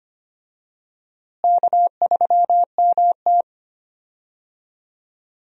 Morse code, heard as a keyed tone, about 25 wpm, 700 Hz. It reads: K3MT
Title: USA Call Signs - 4 Characters - Repeated 2x - 25wpm